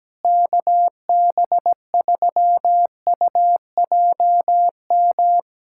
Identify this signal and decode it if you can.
Morse code, heard as a keyed tone, about 17 wpm, 700 Hz. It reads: KB3UJM
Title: USA Callsign DE Callsign 17wpm